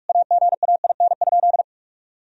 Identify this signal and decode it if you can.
Morse code, heard as a keyed tone, about 46 wpm, 700 Hz. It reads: AGAIN?
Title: Sentences from Top 1000 Words - Review Part 2 - 45wpm